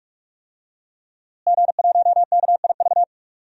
Morse code, heard as a keyed tone, about 46 wpm, 700 Hz. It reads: G1KIV
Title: DXCC-100 Call Signs - 5 Characters - Repeated 2x - 45wpm